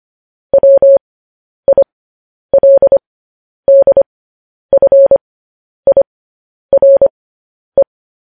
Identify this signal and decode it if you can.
Morse code, heard as a keyed tone, about 25 wpm, spaced as for 10 wpm, 550 Hz. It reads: WILDFIRE
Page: News Headlines in Morse Code at 10 WPM